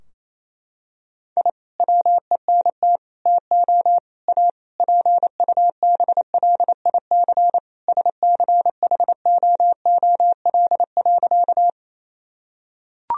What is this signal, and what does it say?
Morse code, heard as a keyed tone, about 27 wpm, 700 Hz. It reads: I WENT TO A PUBLIC SCHOOL.
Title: Sentences from Top 1000 Words - Review Part 3 - 28wpm